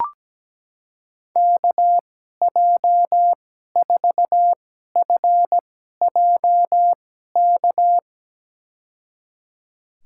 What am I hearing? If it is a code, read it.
Morse code, heard as a keyed tone, about 17 wpm, spaced as for 12 wpm, 700 Hz. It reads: KJ4FJK
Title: USA Call Signs - 6 Characters - 2x Character Spacing - 17wpm